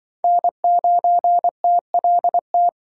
Morse code, heard as a keyed tone, about 24 wpm, 700 Hz. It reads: N9TLT
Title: Speed Racing - USA 1x3 Call Signs 20wpm